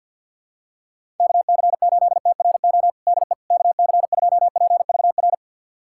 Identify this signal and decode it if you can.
Morse code, heard as a keyed tone, about 53 wpm, 700 Hz. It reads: KC8TRO DE KC2PVR
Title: USA Callsign DE Callsign - 6 Characters - 50wpm